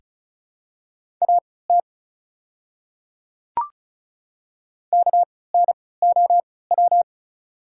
Morse code, heard as a keyed tone, about 34 wpm, spaced as for 20 wpm, 700 Hz. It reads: AT KNOW
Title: 3x Character Spacing - Top 300 Words - 35wpm